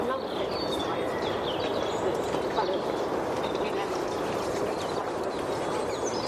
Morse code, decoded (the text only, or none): none